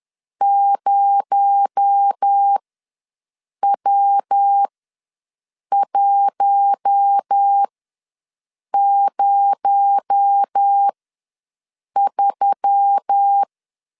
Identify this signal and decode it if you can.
Morse code, heard as a keyed tone, about 10 wpm, spaced as for 6 wpm, 800 Hz. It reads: 0W103
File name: LEZIONI / 6 PKX0 / 6 PKX0 ws 10wpm 4220